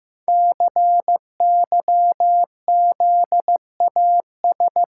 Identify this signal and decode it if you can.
Morse code, heard as a keyed tone, about 15 wpm, 700 Hz. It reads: CYZAS
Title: Code Groups - 5 Letters - 15wpm